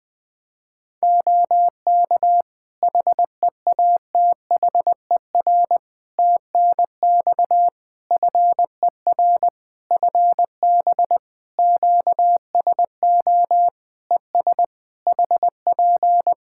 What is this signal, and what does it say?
Morse code, heard as a keyed tone, about 20 wpm, 700 Hz. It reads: OK HEATHER TNX FER FB QSO ES HP
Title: K1USN SST QSO Practice 20wpm